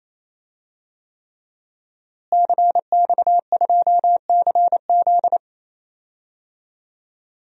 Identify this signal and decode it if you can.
Morse code, heard as a keyed tone, about 28 wpm, 700 Hz. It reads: CX2CZ
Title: DXCC-100 Call Signs - 5 Characters - Repeated 3x - 28wpm